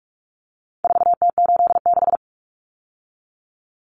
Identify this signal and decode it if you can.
Morse code, heard as a keyed tone, about 44 wpm, 700 Hz. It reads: 4T86
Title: Common Cut Numbers - 4-Digit 45wpm